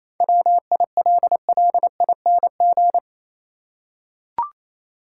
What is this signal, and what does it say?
Morse code, heard as a keyed tone, about 27 wpm, 700 Hz. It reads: WILLING
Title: Seven Letter Words - 28wpm